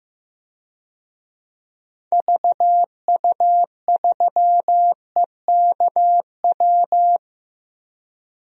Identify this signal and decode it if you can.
Morse code, heard as a keyed tone, about 15 wpm, 700 Hz. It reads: VU3EKW
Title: DXCC-100 Call Signs - 6 Characters - Repeated 3x - 15wpm